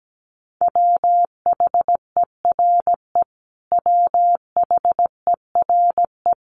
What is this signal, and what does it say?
Morse code, heard as a keyed tone, about 17 wpm, 700 Hz. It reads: WHERE WHERE
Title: Top 500 Words Repeated 17wpm